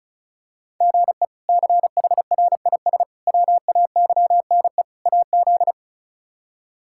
Morse code, heard as a keyed tone, about 34 wpm, 700 Hz. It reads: GE CHRIS WAYNE AZ
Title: K1USN SST QSO Practice - 35wpm